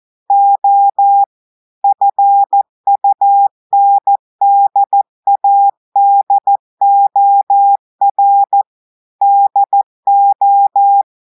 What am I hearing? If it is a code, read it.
Morse code, heard as a keyed tone, about 14 wpm, 800 Hz. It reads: O FUNDADOR DO
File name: audios / portugais / portugais texte en clair 1 vitesse 14 mots minutes